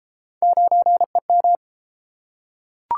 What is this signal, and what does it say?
Morse code, heard as a keyed tone, about 30 wpm, 700 Hz. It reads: 9EM